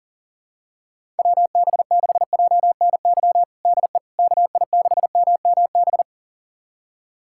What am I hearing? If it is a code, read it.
Morse code, heard as a keyed tone, about 40 wpm, 700 Hz. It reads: WB6JNY DE KI6MMB
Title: USA Callsign DE Callsign 40wpm